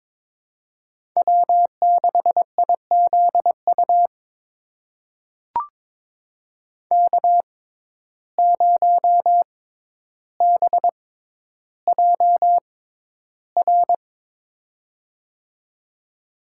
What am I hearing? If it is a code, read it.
Morse code, heard as a keyed tone, about 22 wpm, spaced as for 11 wpm, 700 Hz. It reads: W6IZU K0BJR
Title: USA Call Signs - 6x Character Spacing - 22wpm